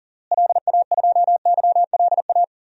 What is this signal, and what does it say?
Morse code, heard as a keyed tone, about 40 wpm, 700 Hz. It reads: LA2YLA
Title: DXCC-100 Call Signs - 6 Characters - Repeated 3x - 40wpm